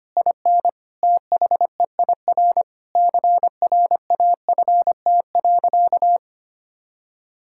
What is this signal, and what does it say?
Morse code, heard as a keyed tone, about 25 wpm, 700 Hz. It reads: IN THEIR CRAFT.
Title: Sentences from Top 1000 Words - Review Part 6 - 25wpm